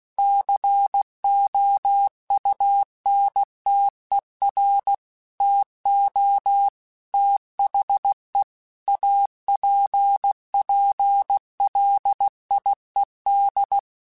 Morse code, heard as a keyed tone, about 16 wpm, 800 Hz. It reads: COUNTER TO THE APPLIED
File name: Technical Subjects / Technical Subjects-mp3s / XL @15